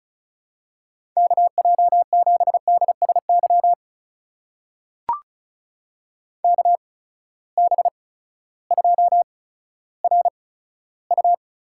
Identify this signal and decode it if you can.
Morse code, heard as a keyed tone, about 36 wpm, spaced as for 16 wpm, 700 Hz. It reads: KJ7DSY KB2RU
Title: USA Call Signs - 6 Characters - 8x Character Spacing - 35wpm